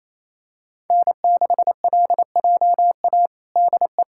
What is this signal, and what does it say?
Morse code, heard as a keyed tone, about 28 wpm, 700 Hz. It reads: N6LJA DE
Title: USA Callsign DE Callsign - 5 Characters - 28wpm